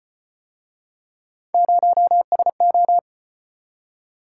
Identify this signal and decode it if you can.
Morse code, heard as a keyed tone, about 38 wpm, 700 Hz. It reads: 0SO